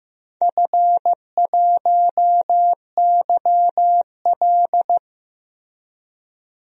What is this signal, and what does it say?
Morse code, heard as a keyed tone, about 15 wpm, 700 Hz. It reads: F1YL
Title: DXCC-100 Call Signs - 4 Characters - Repeated 3x - 15wpm